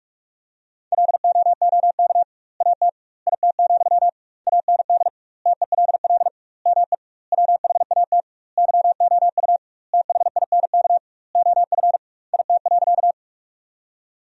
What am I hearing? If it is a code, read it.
Morse code, heard as a keyed tone, about 42 wpm, 700 Hz. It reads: LOOK AT IT, AND TELL ME WHAT YOU THINK OF IT.
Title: 2x Word-Spacing - Sentences from Top 200 Words - 45wpm